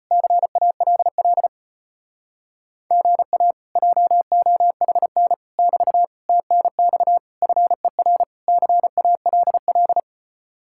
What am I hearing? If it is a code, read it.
Morse code, heard as a keyed tone, about 33 wpm, 700 Hz. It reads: CALL GA JOHN <BT> TNX FER CALL